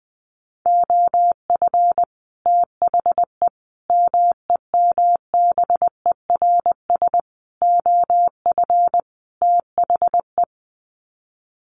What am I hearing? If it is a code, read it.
Morse code, heard as a keyed tone, about 20 wpm, 700 Hz. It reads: OF THE MEMBERS OF THE
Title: Sets of 5 Words Encore2 20wpm